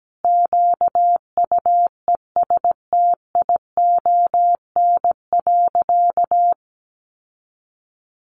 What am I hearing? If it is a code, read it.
Morse code, heard as a keyed tone, about 17 wpm, 700 Hz. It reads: QUESTION.
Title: Sentences from Top 1400 Words 17wpm